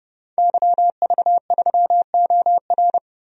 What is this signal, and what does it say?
Morse code, heard as a keyed tone, about 30 wpm, 700 Hz. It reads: YV3OR
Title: DXCC-100 Call Signs - 5 Characters - Repeated 3x - 30wpm